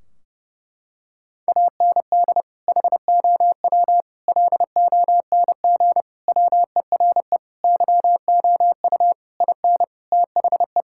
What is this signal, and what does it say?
Morse code, heard as a keyed tone, about 30 wpm, 700 Hz. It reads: AND HOW LONG WERE YOU IN THE
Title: Sentences from Top 1600 Words 30wpm